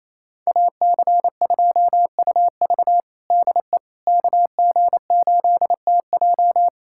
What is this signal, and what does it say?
Morse code, heard as a keyed tone, about 28 wpm, 700 Hz. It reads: AC2UV DE KG8TJ
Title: USA Callsign DE Callsign - 5 Characters - 28wpm